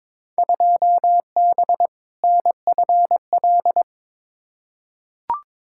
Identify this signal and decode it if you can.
Morse code, heard as a keyed tone, about 22 wpm, 700 Hz. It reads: 2B NFL
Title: Field Day Exchanges Encore - 22wpm